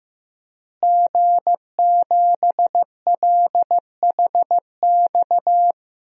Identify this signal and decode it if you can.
Morse code, heard as a keyed tone, about 16 wpm, 700 Hz. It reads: G7LHX
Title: DXCC-100 Call Signs - 5 Characters - Repeated 3x - 15wpm